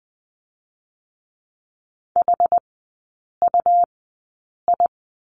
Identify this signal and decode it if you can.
Morse code, heard as a keyed tone, about 20 wpm, spaced as for 8 wpm, 700 Hz. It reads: HUI
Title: 114 Lesson 19e - Letter w - Three Characters 20wpm